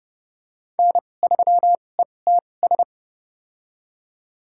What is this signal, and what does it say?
Morse code, heard as a keyed tone, about 31 wpm, spaced as for 22 wpm, 700 Hz. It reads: N3ETS
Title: USA Call Signs - 5 Characters - 2x Character Spacing - 30wpm